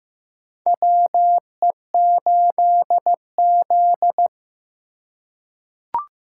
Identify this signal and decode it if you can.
Morse code, heard as a keyed tone, about 16 wpm, 700 Hz. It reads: WE8Z
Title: USA Call Signs - 4 Characters - 4x Character Spacing - 15wpm